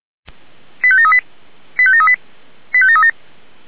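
Five-tone selective call (ZVEI), heard as quick runs of five tones, tones of about 70 ms. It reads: 85738 85738 85737
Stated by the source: ZVEI five-tone call received on SDR.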